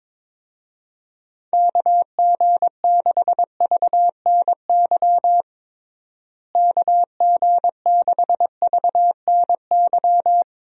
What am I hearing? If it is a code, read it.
Morse code, heard as a keyed tone, about 22 wpm, 700 Hz. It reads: KG6VNY KG6VNY
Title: USA Call Signs - 6 Characters - Repeated 3x - 22wpm